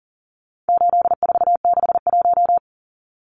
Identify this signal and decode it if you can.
Morse code, heard as a keyed tone, about 40 wpm, 700 Hz. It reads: 8461
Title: Numbers - 4-Digit 40wpm